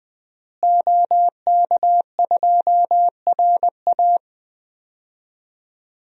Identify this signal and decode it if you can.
Morse code, heard as a keyed tone, about 20 wpm, 700 Hz. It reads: OK2RA
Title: DXCC-100 Call Signs - 5 Characters - Repeated 2x - 20wpm